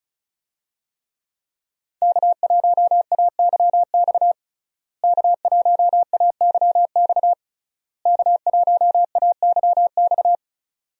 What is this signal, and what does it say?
Morse code, heard as a keyed tone, about 36 wpm, 700 Hz. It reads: K1AYX K1AYX K1AYX
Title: USA Call Signs - 5 Characters - Repeated 3x - 35wpm